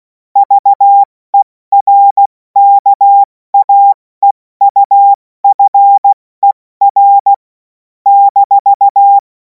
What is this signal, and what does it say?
Morse code, heard as a keyed tone, about 16 wpm, 800 Hz. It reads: VERKAEUFER -
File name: Morsekurs / Witze / Witz 16. Juni 2026 70 F80